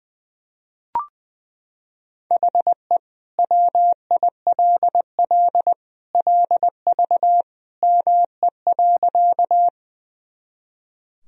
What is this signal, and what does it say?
Morse code, heard as a keyed tone, about 20 wpm, 700 Hz. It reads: HE WILL LV ME.
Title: Sentences from Top 400 Words and Abbreviations 20wpm